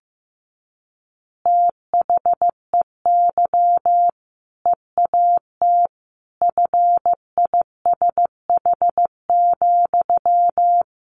Morse code, heard as a keyed tone, about 15 wpm, 700 Hz. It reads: THEY EAT FISH,
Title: Easy Phrases 15wpm